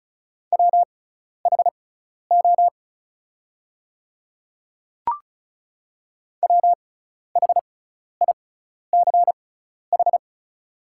Morse code, heard as a keyed tone, about 36 wpm, spaced as for 12 wpm, 700 Hz. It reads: WHO WHICH
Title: 6x Character Spacing - Top 100 Words - 35wpm